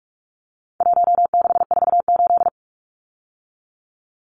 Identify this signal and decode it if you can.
Morse code, heard as a keyed tone, about 44 wpm, 700 Hz. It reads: 1648
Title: Numbers - 4-Digit 45wpm